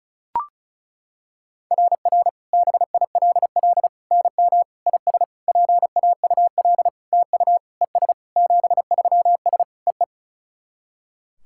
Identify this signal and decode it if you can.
Morse code, heard as a keyed tone, about 36 wpm, 690 Hz. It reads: RR BILL NM IS PAUL TU ES 73S EE
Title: SOTA QSO Practice 35wpm